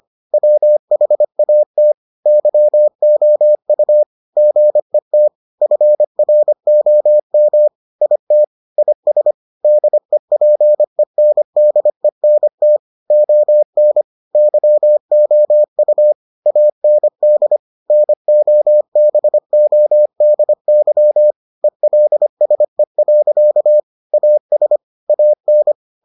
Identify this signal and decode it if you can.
Morse code, heard as a keyed tone, about 25 wpm, 600 Hz. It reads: WHAT YOU GET FROM IT IS DEPENDENT ON YOU AND NOBODY ELSE. AS AN